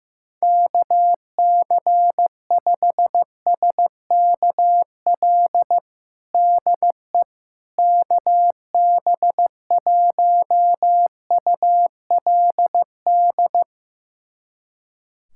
Morse code, heard as a keyed tone, about 15 wpm, 700 Hz. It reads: KC5SKL DE KB1ULD
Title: USA Callsign DE Callsign - 6 Characters - 15wpm